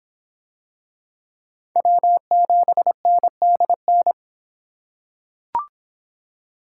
Morse code, heard as a keyed tone, about 24 wpm, 700 Hz. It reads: W7NDN